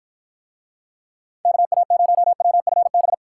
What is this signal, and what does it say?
Morse code, heard as a keyed tone, about 53 wpm, 700 Hz. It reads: KA0WFB